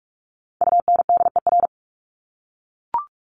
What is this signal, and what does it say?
Morse code, heard as a keyed tone, about 44 wpm, 700 Hz. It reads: UNDER
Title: Top 300 Words 45wpm